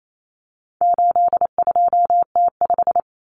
Morse code, heard as a keyed tone, about 28 wpm, 700 Hz. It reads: 82T5
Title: Common Cut Numbers - 4-Digit 28wpm